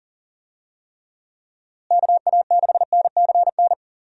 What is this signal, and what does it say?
Morse code, heard as a keyed tone, about 40 wpm, 700 Hz. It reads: KA6NCN
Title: USA Call Signs - 6 Characters - Repeated 2x - 40wpm